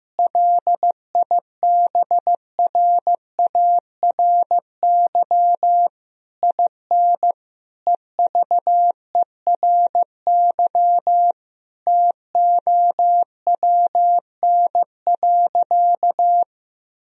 Morse code, heard as a keyed tone, about 15 wpm, 700 Hz. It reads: LIBRARY IN EVERY TOWN.
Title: Sentences from Top 1000 Words - Review Part 4 - 15wpm